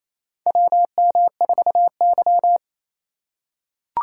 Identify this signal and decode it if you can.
Morse code, heard as a keyed tone, about 28 wpm, 700 Hz. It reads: WM4Y
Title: USA Call Signs - 2x Character Spacing - 28wpm